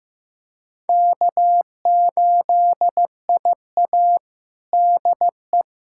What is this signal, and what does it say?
Morse code, heard as a keyed tone, about 15 wpm, 700 Hz. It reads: K8IA DE
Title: Practice QSO - Exchange 4 - Ending - 15wpm